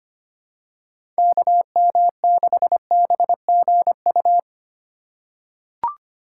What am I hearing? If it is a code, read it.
Morse code, heard as a keyed tone, about 25 wpm, 700 Hz. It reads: KM6BGU
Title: USA Call Signs - 6 Characters - 3x Character Spacing - 25wpm